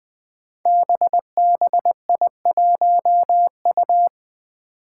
Morse code, heard as a keyed tone, about 20 wpm, 700 Hz. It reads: BBI1U
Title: Five Challenging Letter-Number Combinations 20wpm